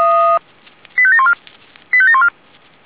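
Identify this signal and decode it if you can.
Five-tone selective call (ZVEI), heard as quick runs of five tones, tones of about 80 ms. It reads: 85714 85714
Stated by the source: ZVEI five-tone call 85724 received by SDR.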